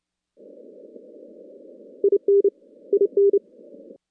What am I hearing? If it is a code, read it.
Morse code, heard as a keyed tone, about 28 wpm, 410 Hz. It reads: IN IN
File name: contests / CW-SS-k5tr- -2100